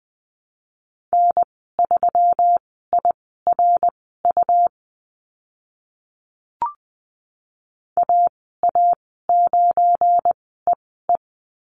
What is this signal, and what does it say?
Morse code, heard as a keyed tone, about 20 wpm, spaced as for 14 wpm, 700 Hz. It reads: N3IRU AA9EE
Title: 179 Lesson 29g - Slash - Call Signs 20wpm